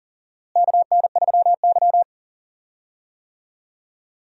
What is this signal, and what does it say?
Morse code, heard as a keyed tone, about 40 wpm, 700 Hz. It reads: KN3Y
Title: USA Call Signs - 4 Characters - Repeated 3x - 40wpm